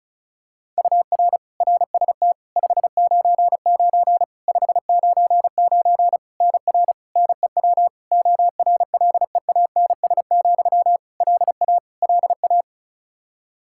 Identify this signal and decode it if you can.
Morse code, heard as a keyed tone, about 36 wpm, 700 Hz. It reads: UR RST 599 599 NR NEW ORLEANS, LA LA
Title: Practice QSO - Exchange 1 - Beginning - 35wpm